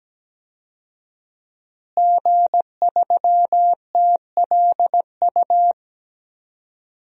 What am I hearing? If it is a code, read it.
Morse code, heard as a keyed tone, about 17 wpm, 700 Hz. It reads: G3TLU
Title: DXCC-100 Call Signs - 5 Characters - Repeated 3x - 17wpm